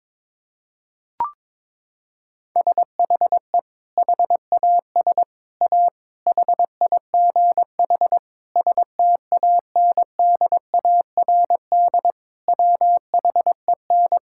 Morse code, heard as a keyed tone, about 23 wpm, 700 Hz. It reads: SHE HAS A HIGH STANDARD WHEN
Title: Sentences from Top 1000 Words - Review Part 6 - 22wpm